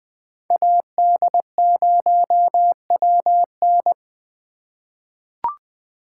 Morse code, heard as a keyed tone, about 20 wpm, 700 Hz. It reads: AD0WN
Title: 2023 POTA Activator Call Signs - 20wpm